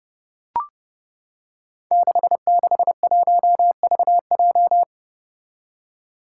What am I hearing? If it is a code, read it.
Morse code, heard as a keyed tone, about 30 wpm, 700 Hz. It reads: 661VJ
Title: Five Challenging Letter-Number Combinations 30wpm